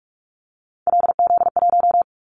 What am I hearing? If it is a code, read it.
Morse code, heard as a keyed tone, about 44 wpm, 700 Hz. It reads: LZ1
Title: Callsign Prefixes for Top 100 DXCC - 45wpm